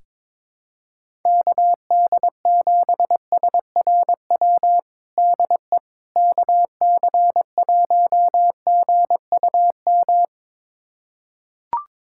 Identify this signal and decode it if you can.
Morse code, heard as a keyed tone, about 22 wpm, 700 Hz. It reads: KD7SRW DE KC1GUM
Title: USA Callsign DE Callsign - 6 Characters - 22wpm